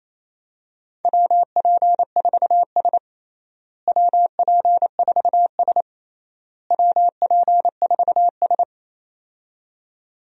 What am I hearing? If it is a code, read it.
Morse code, heard as a keyed tone, about 28 wpm, 700 Hz. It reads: WP4S WP4S WP4S
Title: USA Call Signs - 4 Characters - Repeated 3x - 28wpm